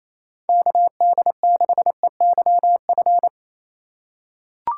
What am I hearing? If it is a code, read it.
Morse code, heard as a keyed tone, about 28 wpm, 700 Hz. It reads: KD6EYF
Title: USA Call Signs - 6 Characters - 3x Character Spacing - 28wpm